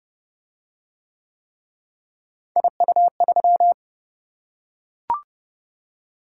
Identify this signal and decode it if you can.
Morse code, heard as a keyed tone, about 28 wpm, 700 Hz. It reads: IU3